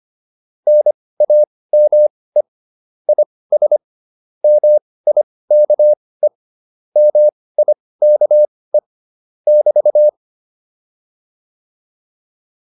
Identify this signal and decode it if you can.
Morse code, heard as a keyed tone, about 25 wpm, spaced as for 18 wpm, 600 Hz. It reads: NAME IS MIKE MIKE <BT>